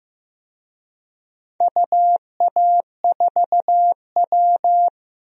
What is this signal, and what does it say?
Morse code, heard as a keyed tone, about 15 wpm, 700 Hz. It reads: UA4W
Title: DXCC-100 Call Signs - 4 Characters - Repeated 2x - 15wpm